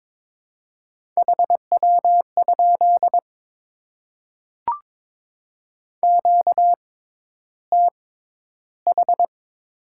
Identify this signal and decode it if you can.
Morse code, heard as a keyed tone, about 22 wpm, spaced as for 11 wpm, 700 Hz. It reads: HW? QTH
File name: Top 20 QSO Elements - 6x Character Spacing - 22wpm